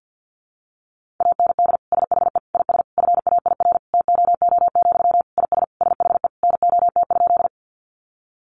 Morse code, heard as a keyed tone, about 50 wpm, 700 Hz. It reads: AND SHE IS FAIR TOO, IS SHE NOT?
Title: Sentences from Top 1300 Words 50wpm